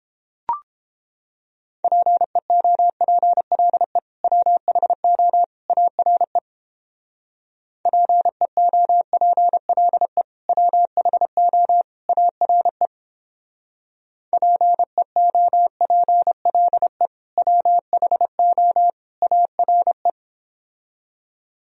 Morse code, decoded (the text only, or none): PEOPLE WHO ARE PEOPLE WHO ARE PEOPLE WHO ARE